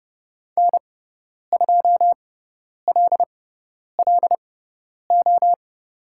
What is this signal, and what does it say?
Morse code, heard as a keyed tone, about 30 wpm, spaced as for 10 wpm, 700 Hz. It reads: N2LLO
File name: Lesson 16g - Letter c - Call Signs 30wpm